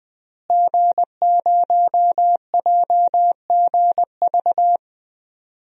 Morse code, heard as a keyed tone, about 21 wpm, 700 Hz. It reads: G0JGV
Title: DXCC-100 Call Signs - 5 Characters - Repeated 3x - 20wpm